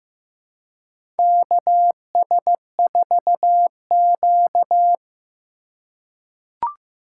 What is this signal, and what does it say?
Morse code, heard as a keyed tone, about 15 wpm, 700 Hz. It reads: KS4Q
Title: USA Call Signs - 4 Characters - 8x Character Spacing - 15wpm